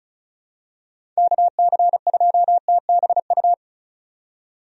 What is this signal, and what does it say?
Morse code, heard as a keyed tone, about 34 wpm, 700 Hz. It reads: KC2TBU
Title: USA Call Signs - 6 Characters - 35wpm